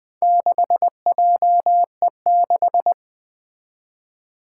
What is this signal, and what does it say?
Morse code, heard as a keyed tone, about 20 wpm, 700 Hz. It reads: 6JE6
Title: Four Challenging Letter-Number Combinations 20wpm